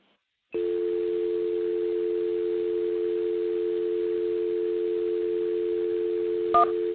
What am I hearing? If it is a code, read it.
Dial tone (North America), 350 + 440 Hz.